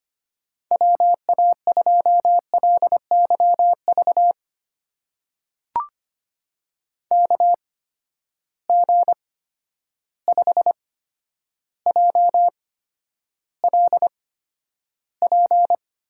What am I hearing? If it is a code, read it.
Morse code, heard as a keyed tone, about 25 wpm, spaced as for 11 wpm, 700 Hz. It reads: WA2LYV KG5JLP
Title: USA Call Signs - 6 Characters - 8x Character Spacing - 25wpm